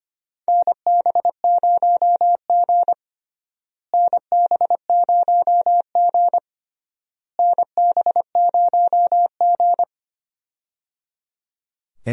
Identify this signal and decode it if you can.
Morse code, heard as a keyed tone, about 25 wpm, 700 Hz. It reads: NB0G NB0G NB0G
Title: USA Call Signs - 4 Characters - Repeated 3x - 25wpm